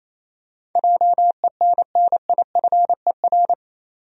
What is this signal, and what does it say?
Morse code, heard as a keyed tone, about 28 wpm, 700 Hz. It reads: JENNIFER
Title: USA Common Names - 28wpm